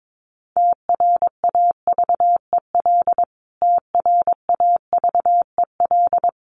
Morse code, heard as a keyed tone, about 22 wpm, 700 Hz. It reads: TRAVEL TRAVEL
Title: Top 500 Words Repeated 22wpm